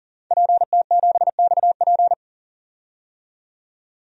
Morse code, heard as a keyed tone, about 40 wpm, 700 Hz. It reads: PT7XP
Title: DXCC-100 Call Signs - 5 Characters - 40wpm